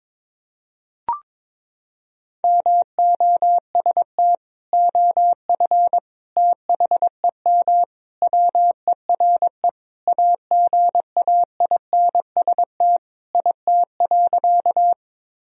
Morse code, heard as a keyed tone, about 22 wpm, 700 Hz. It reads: MOST OF THEM WERE AGAINST IT.
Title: Sentences from Top 1000 Words - Review Part 1 - 22wpm